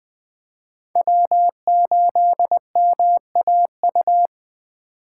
Morse code, heard as a keyed tone, about 20 wpm, 700 Hz. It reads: W8MAU
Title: USA Call Signs - 5 Characters - 4x Character Spacing - 20wpm